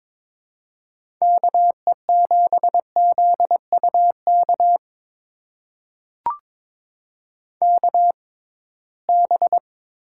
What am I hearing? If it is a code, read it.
Morse code, heard as a keyed tone, about 23 wpm, spaced as for 17 wpm, 700 Hz. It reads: KE7ZUK KB
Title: USA Call Signs - 6 Characters - 6x Character Spacing - 22wpm